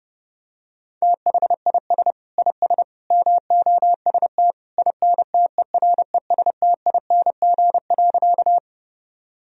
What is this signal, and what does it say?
Morse code, heard as a keyed tone, about 30 wpm, 700 Hz. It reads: THIS IS MOST INTERESTING.
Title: Sentences from Top 1000 Words - Review Part 1 - 30wpm